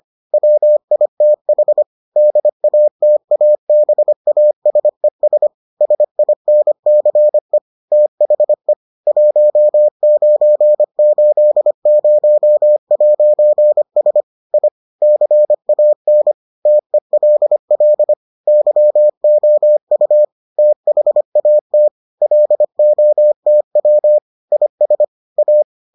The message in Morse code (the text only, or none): WITH DATABASES SINCE THE 1980'S I CAN TELL YOU THAT LOTW IS A